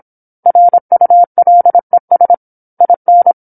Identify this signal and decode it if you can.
Morse code, heard as a keyed tone, about 27 wpm, 700 Hz. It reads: RULES IN
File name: 08-Chapter-08 26 WPM